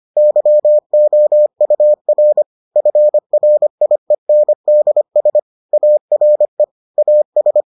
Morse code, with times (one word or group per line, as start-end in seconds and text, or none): 0.2-2.4 YOUR
2.8-5.4 FRIENDS
5.7-6.6 ARE
7.0-7.6 AS